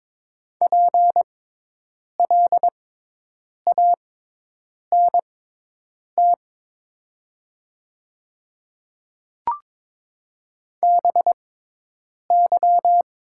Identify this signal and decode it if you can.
Morse code, heard as a keyed tone, about 22 wpm, spaced as for 8 wpm, 700 Hz. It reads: PLANT BY
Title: 6x Character Spacing - Top 300 Words - 22wpm